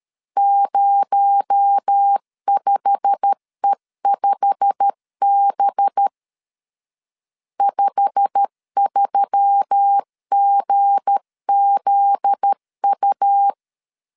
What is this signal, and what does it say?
Morse code, heard as a keyed tone, about 13 wpm, 800 Hz. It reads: 05E5B 53GZU
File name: LEZIONI / 8 YJ45 / 8 YJ45 12wpm 2151